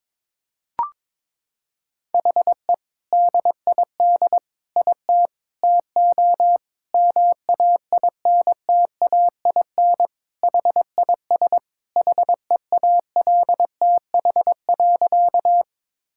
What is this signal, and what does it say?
Morse code, heard as a keyed tone, about 22 wpm, 700 Hz. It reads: HE DID IT TO MAINTAIN HIS HEALTH.
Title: Sentences from Top 1000 Words - Review Part 5 - 22wpm